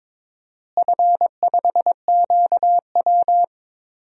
Morse code, heard as a keyed tone, about 22 wpm, 700 Hz. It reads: F5QW
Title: DXCC-100 Call Signs - 4 Characters - Repeated 3x - 22wpm